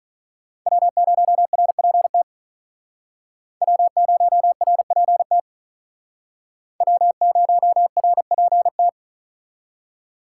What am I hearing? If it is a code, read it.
Morse code, heard as a keyed tone, about 44 wpm, 700 Hz. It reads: W0RPT W0RPT W0RPT